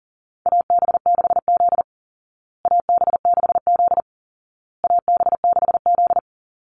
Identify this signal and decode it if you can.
Morse code, heard as a keyed tone, about 40 wpm, 700 Hz. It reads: AB6Z AB6Z AB6Z
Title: Call Signs Repeated 2x - 40wpm